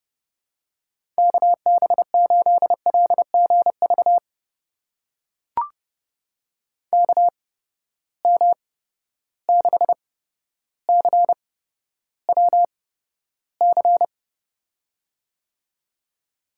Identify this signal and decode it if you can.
Morse code, heard as a keyed tone, about 30 wpm, spaced as for 13 wpm, 700 Hz. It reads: KB8LGV KM6CWC
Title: USA Call Signs - 6 Characters - 8x Character Spacing - 30wpm